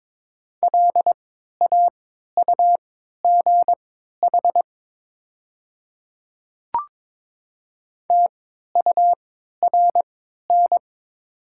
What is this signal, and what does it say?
Morse code, heard as a keyed tone, about 22 wpm, spaced as for 12 wpm, 700 Hz. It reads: LAUGH TURN
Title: 3x Character Spacing - Top 500 Words - 22wpm